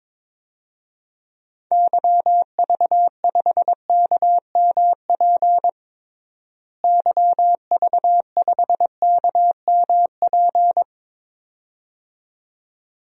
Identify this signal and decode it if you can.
Morse code, heard as a keyed tone, about 23 wpm, 700 Hz. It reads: YV5KMP YV5KMP
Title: DXCC-100 Call Signs - 6 Characters - Repeated 2x - 22wpm